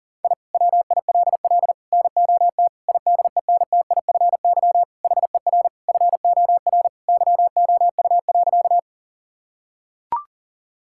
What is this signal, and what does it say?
Morse code, heard as a keyed tone, about 40 wpm, 700 Hz. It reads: I WILL NOT IDENTIFY HER FOR YOU.
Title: Sentences from Top 1000 Words - Review Part 6 - 40wpm